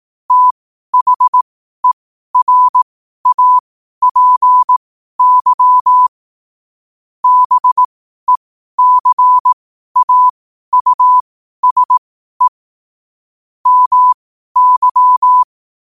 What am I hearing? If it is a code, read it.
Morse code, heard as a keyed tone, about 18 wpm, spaced as for 12 wpm, 1000 Hz. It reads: THERAPY BECAUSE MY
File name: qotdfiles / 12wpm